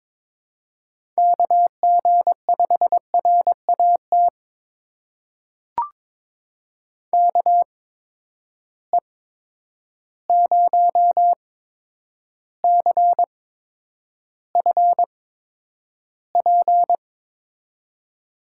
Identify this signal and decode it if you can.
Morse code, heard as a keyed tone, about 22 wpm, spaced as for 9 wpm, 700 Hz. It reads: KG5RAT KE0CFP
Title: USA Call Signs - 6 Characters - 8x Character Spacing - 22wpm